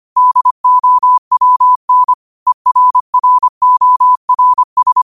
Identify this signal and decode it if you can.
Morse code, heard as a keyed tone, about 25 wpm, 1000 Hz. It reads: DOWN ERRORS
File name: qotdfiles / 25wpm